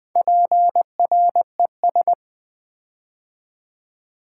Morse code, heard as a keyed tone, about 20 wpm, 700 Hz. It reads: PRES
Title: Build A Word - 4 Letters At A Time - 12-Letter Words - 20wpm